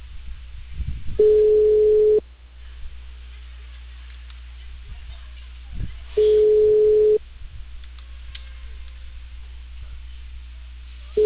Ringback tone (Europe, 425 Hz).